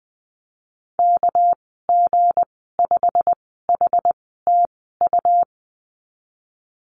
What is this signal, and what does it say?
Morse code, heard as a keyed tone, about 19 wpm, spaced as for 14 wpm, 700 Hz. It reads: KG5HTU
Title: 207 Lesson 33g - Letter j - Call Signs 20wpm